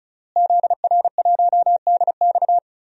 Morse code, heard as a keyed tone, about 34 wpm, 700 Hz. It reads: ZR1DX
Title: DXCC-100 Call Signs - 5 Characters - Repeated 3x - 35wpm